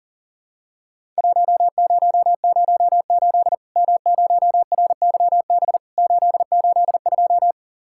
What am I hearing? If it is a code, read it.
Morse code, heard as a keyed tone, about 40 wpm, 700 Hz. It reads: 1008 M0RYB 882
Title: Worked All Europe DX Contest - QTC Exchange - 40wpm